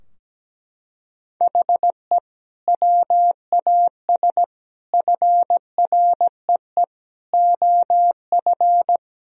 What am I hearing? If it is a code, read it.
Morse code, heard as a keyed tone, about 17 wpm, 700 Hz. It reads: HE WAS FREE OF
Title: Sentences from Top 1000 Words - Review Part 4 - 17wpm